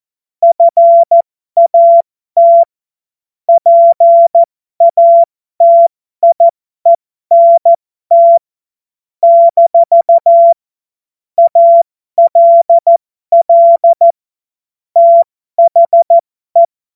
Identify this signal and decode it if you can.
Morse code, heard as a keyed tone, about 14 wpm, spaced as for 12 wpm, 680 Hz. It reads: FAT PATIENT - ALL THE
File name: Morsekurs / Witze